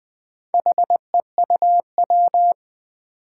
Morse code, heard as a keyed tone, about 20 wpm, 700 Hz. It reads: HEUW
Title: Four Challenging Letter-Number Combinations 20wpm